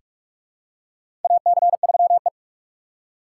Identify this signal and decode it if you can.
Morse code, heard as a keyed tone, about 46 wpm, 700 Hz. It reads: AC3E
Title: USA Call Signs - 4 Characters - 4x Character Spacing - 45wpm